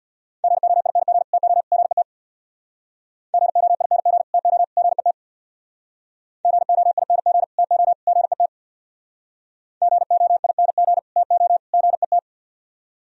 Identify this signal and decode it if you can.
Morse code, heard as a keyed tone, about 64 wpm, 700 Hz. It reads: GOING TO GET GOING TO GET GOING TO GET GOING TO GET